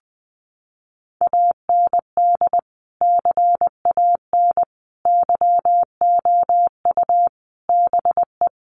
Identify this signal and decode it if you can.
Morse code, heard as a keyed tone, about 20 wpm, 700 Hz. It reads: AND CAN YOU BE
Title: Sentences from Top 1100 Words 20wpm